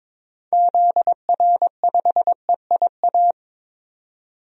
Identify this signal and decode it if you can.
Morse code, heard as a keyed tone, about 22 wpm, 700 Hz. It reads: ZR5EIA
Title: DXCC-100 Call Signs - 6 Characters - 22wpm